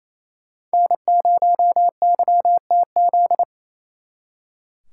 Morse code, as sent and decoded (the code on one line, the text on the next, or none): -. ----- -.-- - --..
N0YTZ